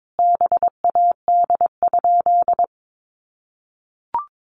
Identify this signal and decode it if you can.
Morse code, heard as a keyed tone, about 22 wpm, 700 Hz. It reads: BAD?
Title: Sentences from Top 700 Words 22wpm